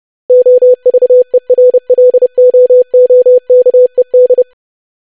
Morse code, heard as a keyed tone, about 30 wpm, 500 Hz. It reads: OVERLOOKED
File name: lesson / web / longwords-25wpm